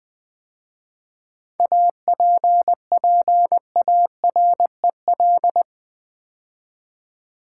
Morse code, heard as a keyed tone, about 20 wpm, 700 Hz. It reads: APPAREL
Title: 7 Letter Words - First 3 Letters Given - 20wpm